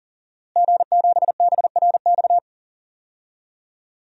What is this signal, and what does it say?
Morse code, heard as a keyed tone, about 40 wpm, 700 Hz. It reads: G7BRX
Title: DXCC-100 Call Signs - 5 Characters - Repeated 2x - 40wpm